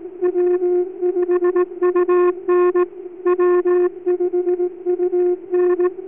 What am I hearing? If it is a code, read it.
Morse code, heard as a keyed tone, about 18 wpm, 360 Hz. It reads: W5UN W5UN